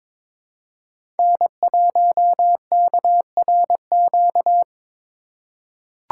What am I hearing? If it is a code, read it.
Morse code, heard as a keyed tone, about 22 wpm, 700 Hz. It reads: N1KRQ
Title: USA Call Signs - 4x Character Spacing - 22wpm